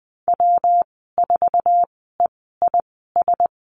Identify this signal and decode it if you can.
Morse code, heard as a keyed tone, about 20 wpm, spaced as for 15 wpm, 700 Hz. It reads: W4EIS
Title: 128 Lesson 21g - Number 6 - Call Signs 20wpm